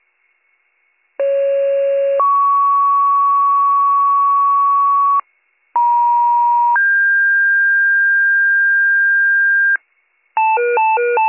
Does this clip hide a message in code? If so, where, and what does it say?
Yes: two-tone page from 1.2 to 9.8 s, heard as a one-second tone, then a three-second tone, about 550 then 1120 Hz, about 950 then 1600 Hz, with noise.